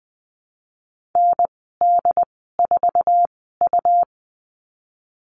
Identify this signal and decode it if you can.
Morse code, heard as a keyed tone, about 20 wpm, spaced as for 15 wpm, 700 Hz. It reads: ND4U
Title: 147 Lesson 24g - Letter y - Call Signs 20wpm